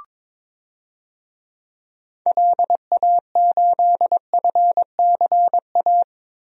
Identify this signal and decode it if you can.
Morse code, heard as a keyed tone, about 22 wpm, 700 Hz. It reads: LA8FCA
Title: DXCC-100 Call Signs - 6 Characters - Repeated 2x - 22wpm